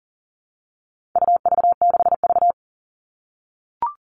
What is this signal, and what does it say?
Morse code, heard as a keyed tone, about 40 wpm, 700 Hz. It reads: UV6V
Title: All Cut Numbers - 4-Digit 40wpm